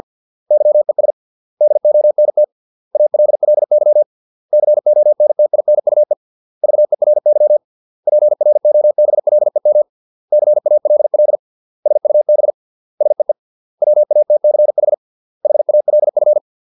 Morse code, heard as a keyed tone, about 50 wpm, 600 Hz. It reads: YES DONT ALLY CONTINUE VERY PROBLEM CALL SUB SEE WATCH HALF